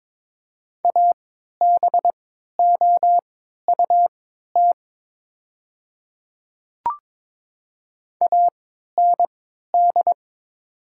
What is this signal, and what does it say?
Morse code, heard as a keyed tone, about 22 wpm, spaced as for 13 wpm, 700 Hz. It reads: ABOUT AND
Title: 3x Character Spacing - Top 200 Words - 22wpm